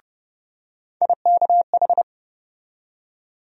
Morse code, heard as a keyed tone, about 32 wpm, 700 Hz. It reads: IKH
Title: USA Call Sign Postfixes - 3 Letters 30wpm